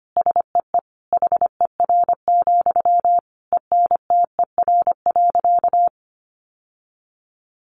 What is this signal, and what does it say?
Morse code, heard as a keyed tone, about 25 wpm, 700 Hz. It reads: SEE HER, ENTER.
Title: Sentences from Top 1300 Words 25wpm